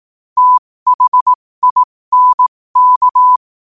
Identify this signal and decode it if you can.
Morse code, heard as a keyed tone, about 18 wpm, spaced as for 15 wpm, 1000 Hz. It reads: THINK
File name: qotdfiles / 15wpm